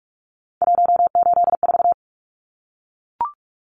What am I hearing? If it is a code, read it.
Morse code, heard as a keyed tone, about 44 wpm, 700 Hz. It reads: 184
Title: Numbers - 3-Digit 45wpm